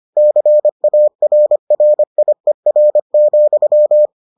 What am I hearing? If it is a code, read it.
Morse code, heard as a keyed tone, about 25 wpm, 600 Hz. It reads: CARRIER,